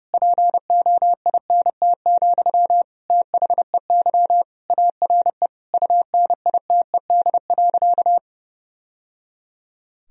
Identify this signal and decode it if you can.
Morse code, heard as a keyed tone, about 30 wpm, 700 Hz. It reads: POINT, THEY ARE UNITED.
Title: Sentences from Top 1000 Words - Review Part 4 - 30wpm